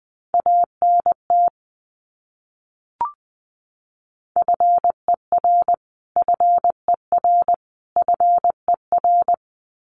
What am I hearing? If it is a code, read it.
Morse code, heard as a keyed tone, about 20 wpm, 700 Hz. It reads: ANT FER FER FER
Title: Abbreviations Repeated 3x - 20wpm